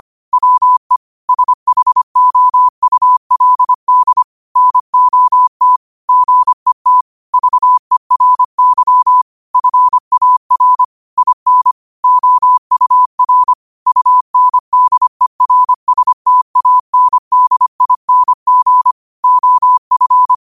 Morse code, heard as a keyed tone, about 25 wpm, 1000 Hz. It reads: WE SHOULD NOT GET VERY FAR IN OUR UNDERSTANDING OF